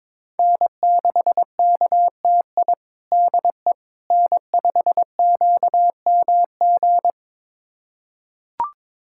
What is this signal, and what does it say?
Morse code, heard as a keyed tone, about 22 wpm, 700 Hz. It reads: N6KTI DE N5QMG
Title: USA Callsign DE Callsign - 5 Characters - 22wpm